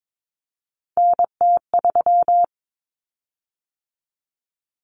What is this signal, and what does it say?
Morse code, heard as a keyed tone, about 22 wpm, 700 Hz. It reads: NT3
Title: Common Cut Numbers - 3-Digit 22wpm